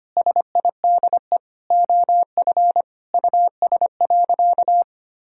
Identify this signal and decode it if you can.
Morse code, heard as a keyed tone, about 25 wpm, 700 Hz. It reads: SIDE OF US.
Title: Sentences from Top 1000 Words - Review Part 3 - 25wpm